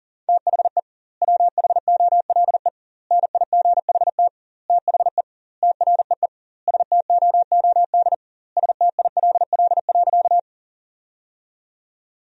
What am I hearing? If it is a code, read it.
Morse code, heard as a keyed tone, about 40 wpm, 700 Hz. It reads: THE WHOLE NIGHT THE TREE STOOD STILL.
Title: 2x Word-Spacing - Sentences from Top 500 Words - 40wpm